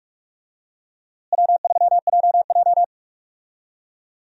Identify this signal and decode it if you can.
Morse code, heard as a keyed tone, about 42 wpm, 700 Hz. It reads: W3JJ
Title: USA Call Signs - 4 Characters - Repeated 3x - 45wpm